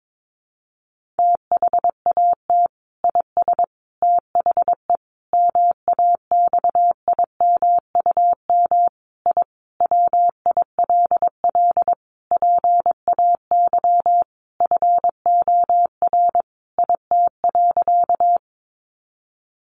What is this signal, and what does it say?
Morse code, heard as a keyed tone, about 22 wpm, 700 Hz. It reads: THAT IS THE MAXIMUM I WILL PAY FOR IT.
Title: Sentences from Top 1300 Words 22wpm